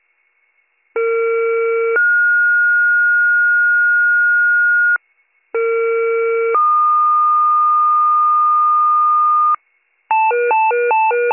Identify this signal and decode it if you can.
Two-tone page, heard as a one-second tone, then a three-second tone, about 460 then 1430 Hz, about 460 then 1220 Hz, with noise. It is heard from 1.0 to 9.6 s.